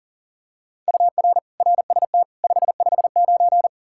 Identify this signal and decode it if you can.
Morse code, heard as a keyed tone, about 37 wpm, 700 Hz. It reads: UR RST 559
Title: Practice QSO - Exchange 1 - Beginning - 40wpm